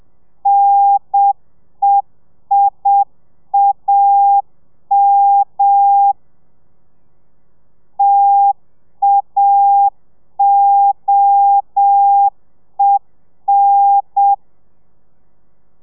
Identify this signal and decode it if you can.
Morse code, heard as a keyed tone, about 7 wpm, 800 Hz. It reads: NEIAM TAOEN